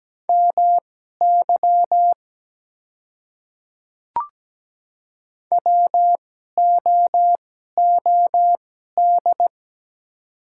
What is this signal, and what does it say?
Morse code, heard as a keyed tone, about 17 wpm, spaced as for 12 wpm, 700 Hz. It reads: MY WOOD
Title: 2x Character Spacing - Top 300 Words - 17wpm